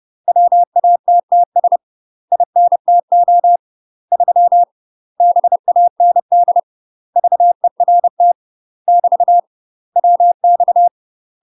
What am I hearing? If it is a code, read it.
Morse code, heard as a keyed tone, about 30 wpm, 700 Hz. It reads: WATTS INTO 3 BAND VERT <BT> WX
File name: UK MASTER QSOs TRAINING / 30WPM / UK MASTER QSOs 0028